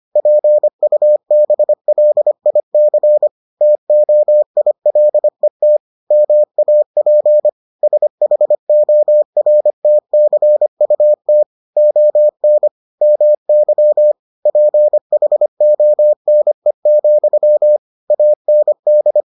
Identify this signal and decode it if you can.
Morse code, heard as a keyed tone, about 25 wpm, 600 Hz. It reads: PUBLIC TOILET MAP SHORTCUT ON MY PHONE, AND